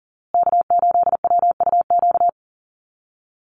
Morse code, heard as a keyed tone, about 40 wpm, 700 Hz. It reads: K8WUQ
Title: USA Call Signs Repeated 3x 40wpm